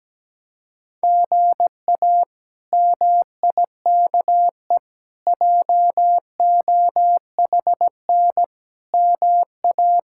Morse code, heard as a keyed tone, about 17 wpm, 700 Hz. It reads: GA MIKE JOHN MA
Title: K1USN SST QSO Practice - 17wpm